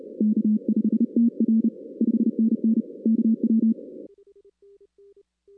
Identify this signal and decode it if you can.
Morse code, heard as a keyed tone, about 32 wpm, 220 Hz. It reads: K5TR 5NN KW
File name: 2016 arrl dx cw / 2016-ARRL-DX-CW-K5TR- - 0-17